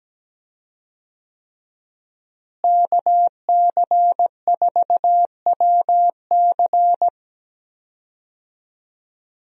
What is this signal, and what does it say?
Morse code, heard as a keyed tone, about 17 wpm, 700 Hz. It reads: KC4WC
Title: USA Call Signs - 5 Characters - Repeated 3x - 17wpm